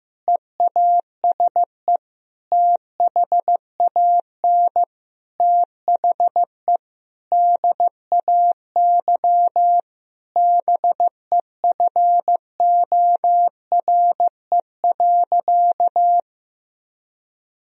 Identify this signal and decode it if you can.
Morse code, heard as a keyed tone, about 15 wpm, 700 Hz. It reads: EASE THAN THE DAY BEFORE.
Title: Sentences from Top 1000 Words - Review Part 1 - 15wpm